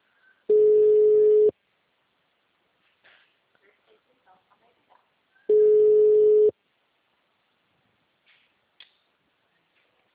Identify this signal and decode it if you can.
Ringback tone (Europe, 425 Hz), with speech.
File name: out- -2004- - - .48230